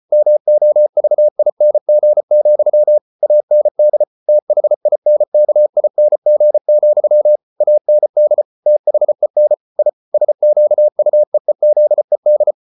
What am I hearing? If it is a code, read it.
Morse code, heard as a keyed tone, about 34 wpm, 600 Hz. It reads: MOVING, AND THINKING, AND THEN I SQUEEZED